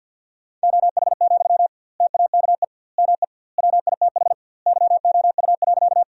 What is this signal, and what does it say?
Morse code, heard as a keyed tone, about 51 wpm, 700 Hz. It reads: OH, TAKE ME WITH YOU.
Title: 2x Word-Spacing - Sentences from Top 500 Words - 50wpm